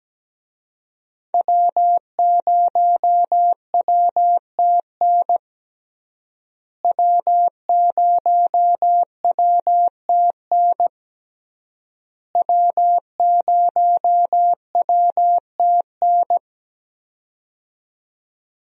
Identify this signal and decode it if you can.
Morse code, heard as a keyed tone, about 17 wpm, 700 Hz. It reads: W0WTN W0WTN W0WTN
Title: USA Call Signs - 5 Characters - Repeated 3x - 17wpm